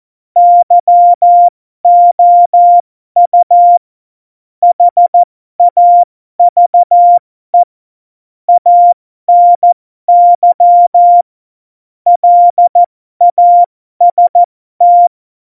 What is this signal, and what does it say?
Morse code, heard as a keyed tone, about 14 wpm, spaced as for 12 wpm, 700 Hz. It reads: YOU HAVE ANY LAST
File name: Morsekurs / Witze